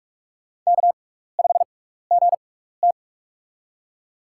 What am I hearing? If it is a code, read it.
Morse code, heard as a keyed tone, about 46 wpm, spaced as for 16 wpm, 700 Hz. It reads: K5GT
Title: USA Call Signs - 4 Characters - 6x Character Spacing - 45wpm